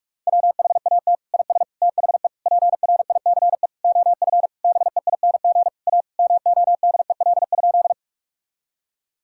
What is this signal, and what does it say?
Morse code, heard as a keyed tone, about 46 wpm, 690 Hz. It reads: WHAT IS THE PRICE OF BEING A MODEL?
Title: Sentences from Top 1000 Words - Review Part 5 - 45wpm